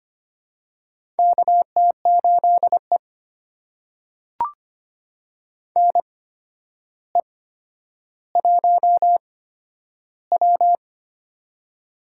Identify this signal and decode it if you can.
Morse code, heard as a keyed tone, about 25 wpm, spaced as for 11 wpm, 700 Hz. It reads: KT8E NE1W
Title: USA Call Signs - 4 Characters - 8x Character Spacing - 25wpm